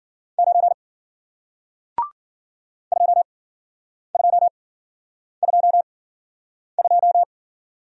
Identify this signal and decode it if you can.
Morse code, heard as a keyed tone, about 54 wpm, spaced as for 9 wpm, 700 Hz. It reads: 9 2222